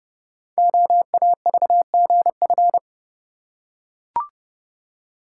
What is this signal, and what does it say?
Morse code, heard as a keyed tone, about 30 wpm, 700 Hz. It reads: OAVGF
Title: Code Groups - 5 Letters - 30wpm